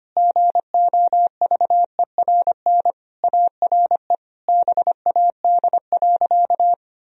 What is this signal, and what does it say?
Morse code, heard as a keyed tone, about 25 wpm, 700 Hz. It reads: GOVERN ARE BAD.
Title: Sentences from Top 1000 Words - Review Part 2 - 25wpm